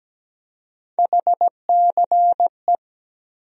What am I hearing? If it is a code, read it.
Morse code, heard as a keyed tone, about 17 wpm, 700 Hz. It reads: HCE
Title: USA Call Sign Postfixes - 3 Letters 17wpm